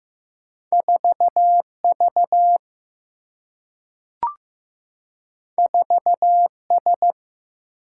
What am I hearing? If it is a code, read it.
Morse code, heard as a keyed tone, about 16 wpm, 700 Hz. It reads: 4V 4S
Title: Single-Letter Single-Number 15wpm